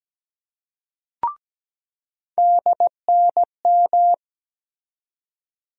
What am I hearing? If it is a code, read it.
Morse code, heard as a keyed tone, about 17 wpm, 700 Hz. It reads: DNM
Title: USA Call Sign Postfixes - 3 Letters 17wpm